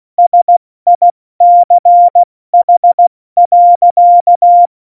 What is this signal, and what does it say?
Morse code, heard as a keyed tone, about 16 wpm, spaced as for 14 wpm, 700 Hz. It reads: SICH.
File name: Morsekurs / Witze / Witz 08. Januar 2019, 70 F80